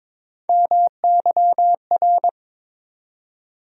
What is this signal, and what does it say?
Morse code, heard as a keyed tone, about 22 wpm, 700 Hz. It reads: MYR
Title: USA Call Sign Postfixes - 3 Letters 22wpm